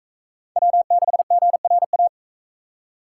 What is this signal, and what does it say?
Morse code, heard as a keyed tone, about 43 wpm, 700 Hz. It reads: W6GRA